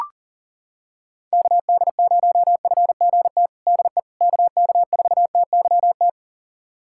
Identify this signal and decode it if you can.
Morse code, heard as a keyed tone, about 40 wpm, 700 Hz. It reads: KD0FGT DE KK4TYT
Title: USA Callsign DE Callsign - 6 Characters - 40wpm